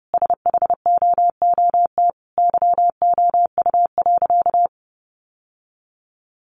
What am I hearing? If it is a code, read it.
Morse code, heard as a keyed tone, about 30 wpm, 700 Hz. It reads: SHOOT YOU.
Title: Sentences from Top 1200 Words 30wpm